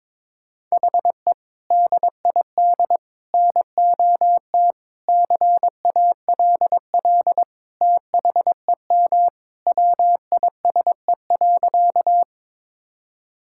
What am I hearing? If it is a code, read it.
Morse code, heard as a keyed tone, about 22 wpm, 700 Hz. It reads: HE DID NOT CALL THEM WISE.
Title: Sentences from Top 1600 Words 22wpm